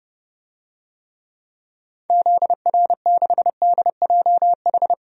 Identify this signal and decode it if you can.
Morse code, heard as a keyed tone, about 30 wpm, 700 Hz. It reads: ZR6DJH
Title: DXCC-100 Call Signs - 6 Characters - Repeated 2x - 30wpm